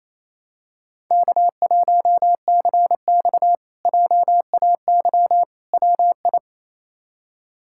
Morse code, heard as a keyed tone, about 28 wpm, 700 Hz. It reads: K1CX JAY WI
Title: K1USN SST QSO Practice - 28wpm